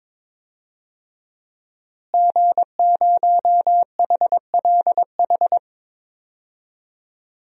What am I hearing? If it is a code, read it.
Morse code, heard as a keyed tone, about 22 wpm, 700 Hz. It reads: G0HLH
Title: DXCC-100 Call Signs - 5 Characters - Repeated 3x - 22wpm